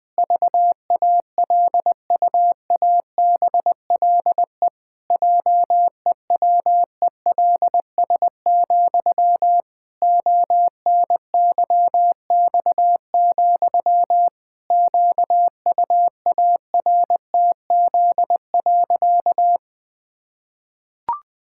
Morse code, heard as a keyed tone, about 20 wpm, 700 Hz. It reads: VALUABLE JEWELS, ONYX, QUARTZ.
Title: Pangram Challenge v2 - 20wpm